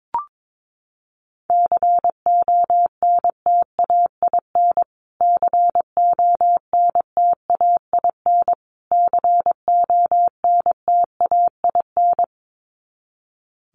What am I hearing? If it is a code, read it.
Morse code, heard as a keyed tone, about 22 wpm, 700 Hz. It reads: CONTAIN CONTAIN CONTAIN
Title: Top 500 Words Repeated 22wpm